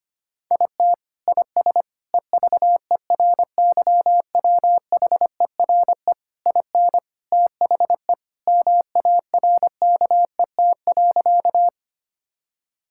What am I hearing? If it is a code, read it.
Morse code, heard as a keyed tone, about 25 wpm, 700 Hz. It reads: IT IS EVERYWHERE IN THE MARKET.
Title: Sentences from Top 1600 Words 25wpm